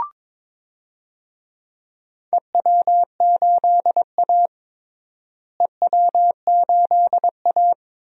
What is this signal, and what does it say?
Morse code, heard as a keyed tone, about 22 wpm, 700 Hz. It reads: EW8A EW8A
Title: DXCC-100 Call Signs - 4 Characters - Repeated 2x - 22wpm